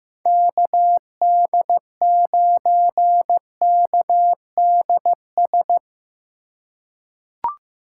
Morse code, heard as a keyed tone, about 15 wpm, 700 Hz. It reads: KD9KDS
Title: USA Call Signs - 6 Characters - 3x Character Spacing - 15wpm